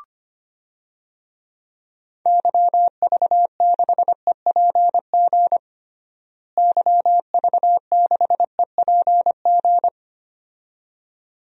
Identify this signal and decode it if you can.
Morse code, heard as a keyed tone, about 25 wpm, 700 Hz. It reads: YV6EPG YV6EPG
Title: DXCC-100 Call Signs - 6 Characters - Repeated 2x - 25wpm